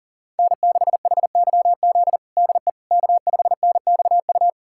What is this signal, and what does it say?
Morse code, heard as a keyed tone, about 37 wpm, 700 Hz. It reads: N6HYZ DE K5NXU
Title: USA Callsign DE Callsign - 5 Characters - 40wpm